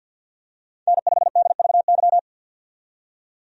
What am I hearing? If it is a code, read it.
Morse code, heard as a keyed tone, about 51 wpm, 700 Hz. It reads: N5DVY
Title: USA Call Signs - 5 Characters - 50wpm